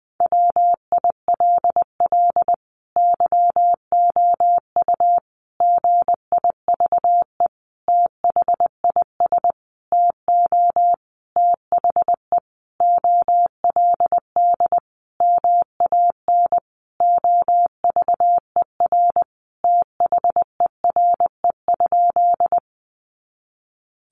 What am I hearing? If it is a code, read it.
Morse code, heard as a keyed tone, about 20 wpm, 700 Hz. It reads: WILL YOU GIVE THIS TO THE OLD MAN OVER THERE?
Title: Sentences from Top 200 Words 20wpm